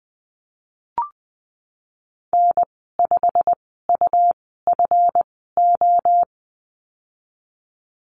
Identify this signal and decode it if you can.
Morse code, heard as a keyed tone, about 20 wpm, spaced as for 15 wpm, 700 Hz. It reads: N5UFO
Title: 167 Lesson 27g - Number 7 - Call Signs 20wpm